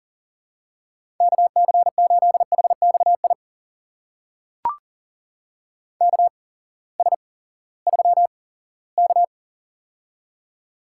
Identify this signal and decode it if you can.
Morse code, heard as a keyed tone, about 42 wpm, spaced as for 20 wpm, 700 Hz. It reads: KC8HXI KS3K
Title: USA Call Signs - 8x Character Spacing - 40wpm